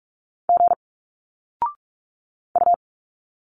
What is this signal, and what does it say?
Morse code, heard as a keyed tone, about 44 wpm, 700 Hz. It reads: G U
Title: Single Letters 45wpm